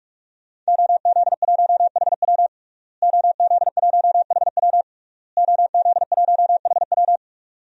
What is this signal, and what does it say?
Morse code, heard as a keyed tone, about 46 wpm, 700 Hz. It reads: OZ1HW OZ1HW OZ1HW
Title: DXCC-100 Call Signs - 5 Characters - Repeated 3x - 45wpm